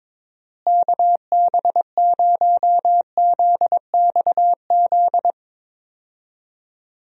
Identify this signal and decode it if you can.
Morse code, heard as a keyed tone, about 21 wpm, 700 Hz. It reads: KB0ZXZ
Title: USA Call Signs - 6 Characters - Repeated 3x - 22wpm